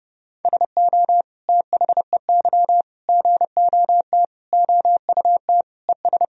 Morse code, heard as a keyed tone, about 28 wpm, 700 Hz. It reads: SO THEY GOT OUT ES
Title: Sentences from Top 300 Words and Abbreviations 30wpm